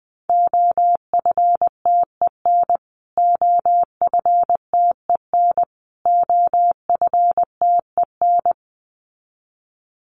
Morse code, heard as a keyed tone, about 20 wpm, 700 Hz. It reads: OFTEN OFTEN OFTEN
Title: Top 300 Words Repeated 20wpm